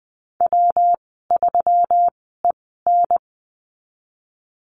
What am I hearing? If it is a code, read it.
Morse code, heard as a keyed tone, about 19 wpm, spaced as for 14 wpm, 700 Hz. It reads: W3EN
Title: 154 Lesson 25g - Letter p - Call Signs 20wpm